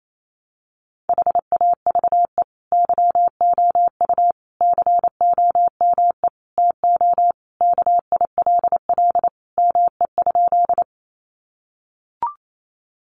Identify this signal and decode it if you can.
Morse code, heard as a keyed tone, about 28 wpm, 700 Hz. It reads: HAVE YOU COME TO KILL ME?
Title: Sentences from Top 1400 Words 28wpm